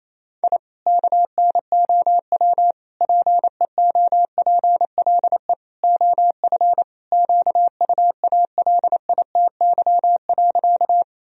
Morse code, heard as a keyed tone, about 27 wpm, 700 Hz. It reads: I KNOW PEOPLE OF QUALITY.
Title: Sentences from Top 1000 Words - Review Part 4 - 28wpm